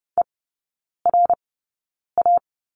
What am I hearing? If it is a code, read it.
Morse code, heard as a keyed tone, about 30 wpm, spaced as for 9 wpm, 700 Hz. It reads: ERA
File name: Lesson 18e - Letter m - Three Characters 30wpm